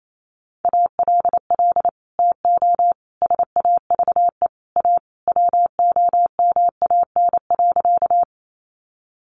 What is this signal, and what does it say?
Morse code, heard as a keyed tone, about 28 wpm, 700 Hz. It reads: ALL TO SAVE A WOMAN.
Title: Sentences from Top 1100 Words 28wpm